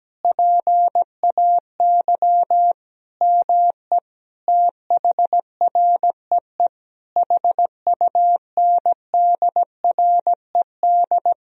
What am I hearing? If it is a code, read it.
Morse code, heard as a keyed tone, about 17 wpm, 700 Hz. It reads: PAY ME THREE HUNDRED
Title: Sentences from Top 1000 Words - Review Part 2 - 17wpm